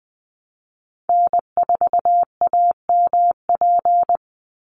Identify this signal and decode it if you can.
Morse code, heard as a keyed tone, about 20 wpm, 700 Hz. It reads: N4AMP
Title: USA Call Signs 20wpm